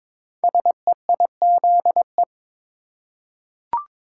Morse code, heard as a keyed tone, about 23 wpm, 700 Hz. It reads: SEIZE
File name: 5 Letter Words - First 3 Letters Given - 22wpm